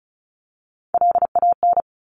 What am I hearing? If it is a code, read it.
Morse code, heard as a keyed tone, about 36 wpm, 690 Hz. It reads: LAN
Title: Top Three Letter Combinations in Words 35wpm